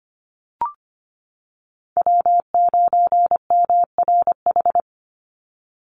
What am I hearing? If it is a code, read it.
Morse code, heard as a keyed tone, about 25 wpm, 700 Hz. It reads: W9MRH
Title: NA SOTA Chaser Callsigns 25wpm